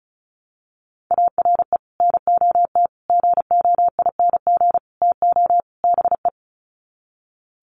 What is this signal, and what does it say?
Morse code, heard as a keyed tone, about 34 wpm, 700 Hz. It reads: ARE NOT GOING TO BE
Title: Sets of 5 Words Encore 35wpm